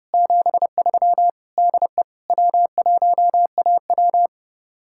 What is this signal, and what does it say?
Morse code, heard as a keyed tone, about 35 wpm, 700 Hz. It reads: 73 DE W1AW
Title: Speed-Racing Common QSO Phrases Encore - 30wpm